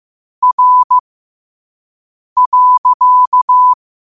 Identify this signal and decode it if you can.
Morse code, heard as a keyed tone, about 15 wpm, spaced as for 5 wpm, 1000 Hz. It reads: R.
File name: qotdfiles / 5wpm